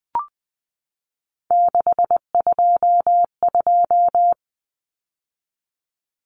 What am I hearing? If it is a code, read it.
Morse code, heard as a keyed tone, about 20 wpm, 700 Hz. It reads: 622
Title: Common Cut Numbers - 3-Digit 20wpm